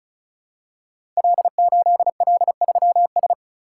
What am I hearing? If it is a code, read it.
Morse code, heard as a keyed tone, about 36 wpm, 700 Hz. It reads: L8L3S
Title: Code Groups - 5 Letter:Numbers - 35wpm